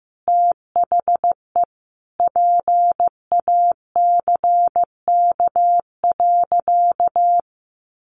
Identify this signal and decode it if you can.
Morse code, heard as a keyed tone, about 15 wpm, 700 Hz. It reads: THE PACK.
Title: Sentences from Top 1300 Words 15wpm